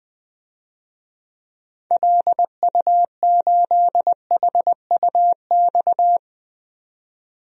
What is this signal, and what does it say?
Morse code, heard as a keyed tone, about 20 wpm, 700 Hz. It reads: LU8HUX
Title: DXCC-100 Call Signs - 6 Characters - 20wpm